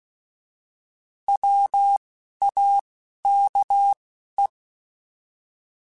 Morse code, heard as a keyed tone, about 16 wpm, 800 Hz. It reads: W A K E
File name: Alphabetical / Alphabetical mp3s / Ws @10